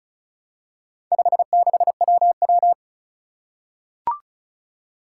Morse code, heard as a keyed tone, about 36 wpm, 700 Hz. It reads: 56WW
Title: Four Challenging Letter-Number Combinations 35wpm